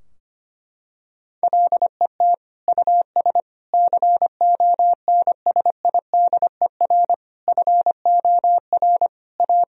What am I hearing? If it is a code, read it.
Morse code, heard as a keyed tone, about 25 wpm, 700 Hz. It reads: LET US CONSIDER FOR A
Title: Sentences from Top 1000 Words - Review Part 4 - 25wpm